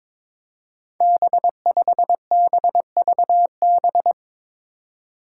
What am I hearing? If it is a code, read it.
Morse code, heard as a keyed tone, about 22 wpm, 700 Hz. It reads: B5BVB
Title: Five Challenging Letter-Number Combinations 22wpm